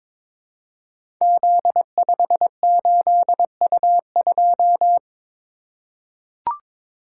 Morse code, heard as a keyed tone, about 22 wpm, 700 Hz. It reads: Z58U2
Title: Code Groups - 5 Letter:Numbers - 22wpm